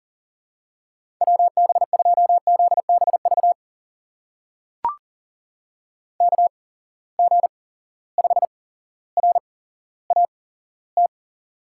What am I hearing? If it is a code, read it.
Morse code, heard as a keyed tone, about 40 wpm, spaced as for 17 wpm, 700 Hz. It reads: WB2ZBV KG5RAT
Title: USA Call Signs - 6 Characters - 8x Character Spacing - 40wpm